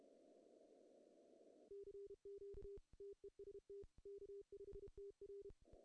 Morse code, heard as a keyed tone, about 32 wpm, 400 Hz. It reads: CQ TEST K5TR
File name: CW / cqww-cw-k5tr